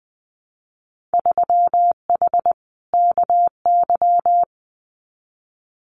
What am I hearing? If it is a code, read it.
Morse code, heard as a keyed tone, about 20 wpm, 700 Hz. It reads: 3H KY
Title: Winter Field Day Exchanges Encore 20wpm